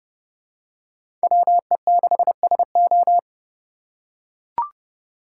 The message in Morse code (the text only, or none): WE6SO